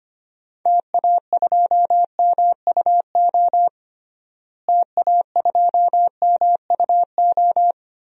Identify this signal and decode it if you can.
Morse code, heard as a keyed tone, about 25 wpm, 700 Hz. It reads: TA2MUO TA2MUO
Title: DXCC-100 Call Signs - 6 Characters - Repeated 3x - 25wpm